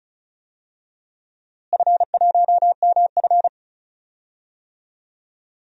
Morse code, heard as a keyed tone, about 36 wpm, 700 Hz. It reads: F1MF
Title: DXCC-100 Call Signs - 4 Characters - Repeated 2x - 35wpm